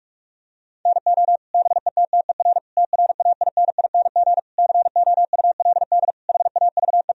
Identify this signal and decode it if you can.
Morse code, heard as a keyed tone, about 43 wpm, 700 Hz. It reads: NO BETTER TRAINING COULD HAVE
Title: Sentences from Top 1000 Words - Review Part 4 - 45wpm